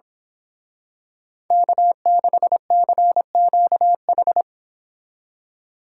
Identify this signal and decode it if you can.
Morse code, heard as a keyed tone, about 26 wpm, 700 Hz. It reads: K6CQH
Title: Speed Racing - USA 1x3 Call Signs 22wpm